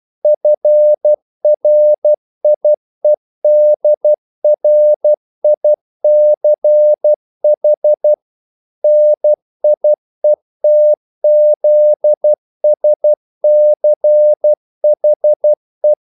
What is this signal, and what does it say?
Morse code, heard as a keyed tone, about 12 wpm, 600 Hz. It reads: FRIEDRICH NIETZSCHE
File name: CW / Practice / EnglishQuotations0112